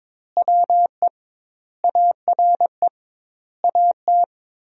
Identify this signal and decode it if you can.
Morse code, heard as a keyed tone, about 22 wpm, 700 Hz. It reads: WE ARE AT
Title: 2x Word-Spacing - Sentences from Top 300 Words - 22wpm